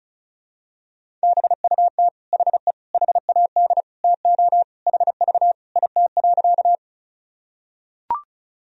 Morse code, heard as a keyed tone, about 34 wpm, 700 Hz. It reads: BUT HE HAD TO HV IT.
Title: Sentences from Top 100 Words and Abbreviations 35wpm